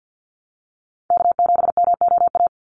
Morse code, heard as a keyed tone, about 50 wpm, 700 Hz. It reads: K7MOA
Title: Call Signs - 50wpm